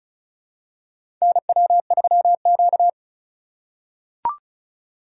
Morse code, heard as a keyed tone, about 36 wpm, 700 Hz. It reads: NW3Q
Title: USA Call Signs - 4 Characters - 4x Character Spacing - 35wpm